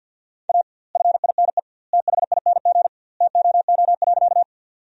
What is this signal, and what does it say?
Morse code, heard as a keyed tone, about 52 wpm, 700 Hz. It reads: A FINE THING TOO.
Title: 2x Word-Spacing - Sentences from Top 500 Words - 50wpm